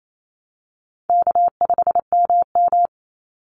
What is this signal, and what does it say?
Morse code, heard as a keyed tone, about 28 wpm, 700 Hz. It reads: K5MM
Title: Easy Call Signs Repeated 2x - 28wpm